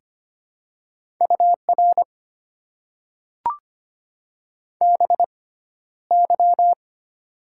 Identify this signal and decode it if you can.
Morse code, heard as a keyed tone, about 25 wpm, spaced as for 13 wpm, 700 Hz. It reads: UR BY
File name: Top 90 QSO Elements - 6x Character Spacing - 25wpm